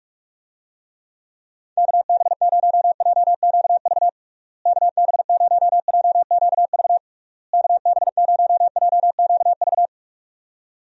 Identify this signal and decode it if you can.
Morse code, heard as a keyed tone, about 43 wpm, 700 Hz. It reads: KB0JQV KB0JQV KB0JQV
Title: USA Call Signs - 6 Characters - Repeated 3x - 45wpm